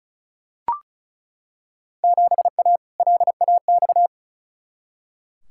Morse code, heard as a keyed tone, about 34 wpm, 700 Hz. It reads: 7A LAX
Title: Field Day Exchanges - 35wpm